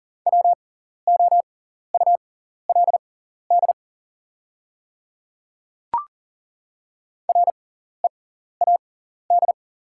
Morse code, heard as a keyed tone, about 40 wpm, spaced as for 14 wpm, 700 Hz. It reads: WOULD READ
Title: 6x Character Spacing - Top 200 Words - 40wpm